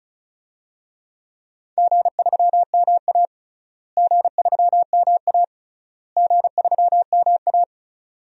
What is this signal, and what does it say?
Morse code, heard as a keyed tone, about 34 wpm, 700 Hz. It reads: G3MA G3MA G3MA
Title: DXCC-100 Call Signs - 4 Characters - Repeated 3x - 35wpm